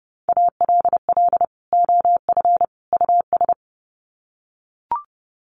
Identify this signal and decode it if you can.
Morse code, heard as a keyed tone, about 30 wpm, 700 Hz. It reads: ALL OF US
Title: Sets of 3 Words 30wpm